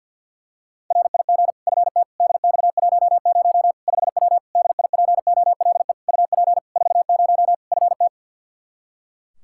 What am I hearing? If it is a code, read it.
Morse code, heard as a keyed tone, about 51 wpm, 700 Hz. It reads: RIG FT DX10 5W DIPOLE UP 40 FT
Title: Practice QSO - Exchange 2 - Info - 50wpm